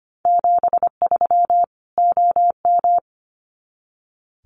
Morse code, heard as a keyed tone, about 25 wpm, 700 Hz. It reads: 73 OM
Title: Common QSO Phrases 25wpm